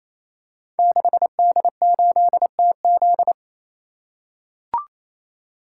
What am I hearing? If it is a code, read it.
Morse code, heard as a keyed tone, about 28 wpm, 700 Hz. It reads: 6D8TZ
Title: Code Groups - 5 Letter:Numbers - 28wpm